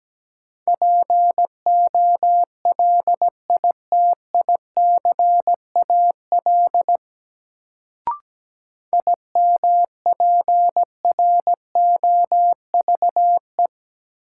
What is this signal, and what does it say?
Morse code, heard as a keyed tone, about 17 wpm, 700 Hz. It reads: POLITICAL IMPROVE
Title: Sentences from Top 800 Words - New Word Drill - 17wpm